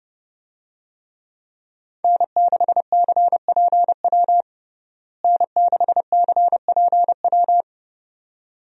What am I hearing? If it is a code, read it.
Morse code, heard as a keyed tone, about 30 wpm, 700 Hz. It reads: N6CPW N6CPW
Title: USA Call Signs - 5 Characters - Repeated 2x - 30wpm